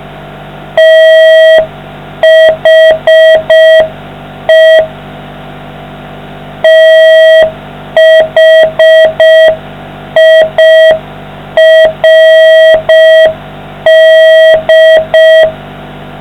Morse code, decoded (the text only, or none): THE THIRD